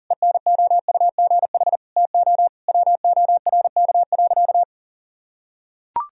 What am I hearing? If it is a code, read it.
Morse code, heard as a keyed tone, about 40 wpm, 700 Hz. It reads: ENOUGH TO WORK.
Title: Sentences from Top 1000 Words - Review Part 1 - 40wpm